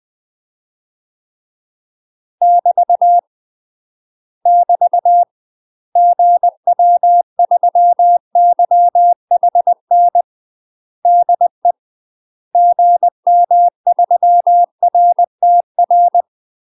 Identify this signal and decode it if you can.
Morse code, heard as a keyed tone, about 20 wpm, 700 Hz. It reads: <BT> <BT> GW3YHN DE GM3RTR
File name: UK MASTER QSOs TRAINING / 20WPM / UK MASTER QSOs 0051